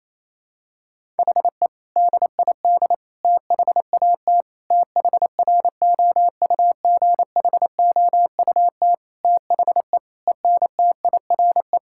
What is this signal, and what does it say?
Morse code, heard as a keyed tone, about 28 wpm, 700 Hz. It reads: HE DID THAT THROUGHOUT THE ENTIRE
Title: Sentences from Top 1000 Words - Review Part 5 - 28wpm